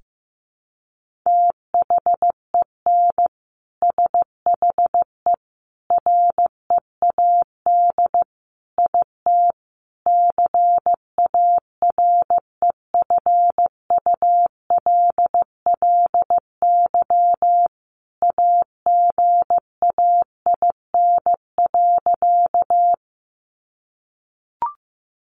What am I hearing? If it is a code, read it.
Morse code, heard as a keyed tone, about 15 wpm, 700 Hz. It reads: THEN SHE READ IT CAREFULLY AGAIN.
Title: Sentences from Top 1300 Words 15wpm